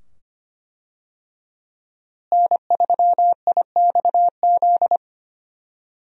Morse code, heard as a keyed tone, about 25 wpm, 700 Hz. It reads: N3IXZ
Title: USA Call Signs - 5 Characters - Repeated 2x - 25wpm